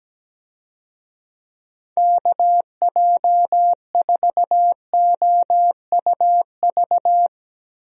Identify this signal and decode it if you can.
Morse code, heard as a keyed tone, about 17 wpm, 700 Hz. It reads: KJ4OUV
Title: USA Call Signs - 6 Characters - Repeated 3x - 17wpm